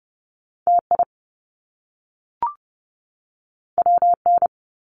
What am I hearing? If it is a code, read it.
Morse code, heard as a keyed tone, about 30 wpm, 700 Hz. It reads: TI WN
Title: Top Two Letter Combinations in Words 30wpm